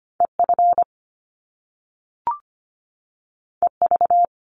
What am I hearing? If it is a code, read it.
Morse code, heard as a keyed tone, about 25 wpm, 700 Hz. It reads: EF EV
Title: Top Two Letter Combinations in Words 25wpm